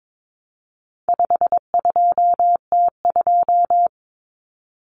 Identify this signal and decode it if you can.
Morse code, heard as a keyed tone, about 22 wpm, 700 Hz. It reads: 52T2
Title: Common Cut Numbers - 4-Digit 22wpm